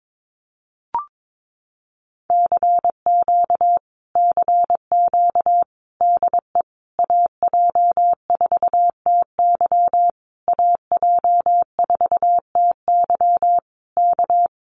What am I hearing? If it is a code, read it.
Morse code, heard as a keyed tone, about 22 wpm, 700 Hz. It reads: CQ CQ DE AJ4TY AJ4TY K
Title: Common QSO Phrases Encore 22wpm